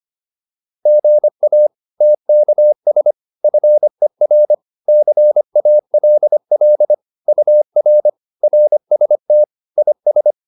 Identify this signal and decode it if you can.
Morse code, heard as a keyed tone, about 25 wpm, 600 Hz. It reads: GA TKS FER CALL UR RST IS